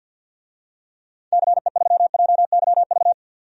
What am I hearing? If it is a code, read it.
Morse code, heard as a keyed tone, about 47 wpm, 700 Hz. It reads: CE3JYV
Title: DXCC-100 Call Signs - 6 Characters - 50wpm